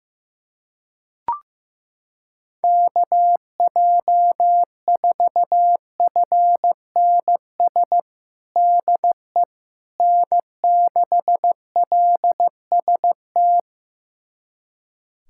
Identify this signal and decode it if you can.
Morse code, heard as a keyed tone, about 15 wpm, 700 Hz. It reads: KJ4FNS DE N6LST
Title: USA Callsign DE Callsign 15wpm